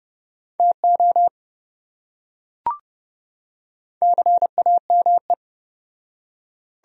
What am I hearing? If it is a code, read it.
Morse code, heard as a keyed tone, about 28 wpm, 700 Hz. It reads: TO CAME